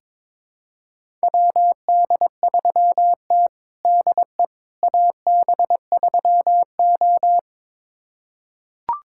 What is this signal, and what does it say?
Morse code, heard as a keyed tone, about 22 wpm, 700 Hz. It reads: WD3T DE AB3O
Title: USA Callsign DE Callsign - 4 Characters - 22wpm